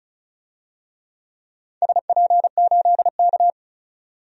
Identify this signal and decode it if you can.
Morse code, heard as a keyed tone, about 36 wpm, 690 Hz. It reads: SP8K
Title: DXCC-100 Call Signs - 4 Characters - 35wpm